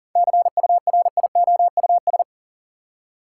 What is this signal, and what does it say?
Morse code, heard as a keyed tone, about 40 wpm, 700 Hz. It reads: CURIOUS
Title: Seven Letter Words - 40wpm